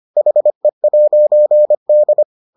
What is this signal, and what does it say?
Morse code, heard as a keyed tone, about 25 wpm, 600 Hz. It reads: HE'D